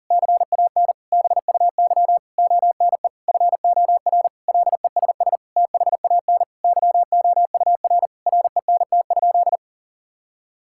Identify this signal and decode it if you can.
Morse code, heard as a keyed tone, about 40 wpm, 700 Hz. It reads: CAN BUY ONE FOR LESS THAN YOUR RENT?
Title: Sentences from Top 1000 Words - Review Part 5 - 40wpm